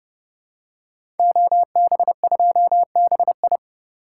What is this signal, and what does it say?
Morse code, heard as a keyed tone, about 30 wpm, 700 Hz. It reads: OB2BI
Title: Code Groups - 5 Letter:Numbers - 30wpm